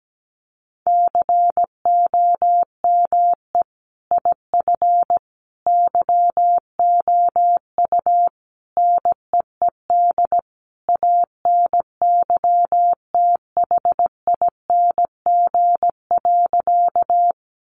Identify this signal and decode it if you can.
Morse code, heard as a keyed tone, about 17 wpm, 700 Hz. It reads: COME IF YOU NEED ANYTHING.
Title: Sentences from Top 1000 Words 17wpm